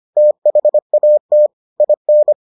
Morse code, heard as a keyed tone, about 25 wpm, 600 Hz. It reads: THAT IN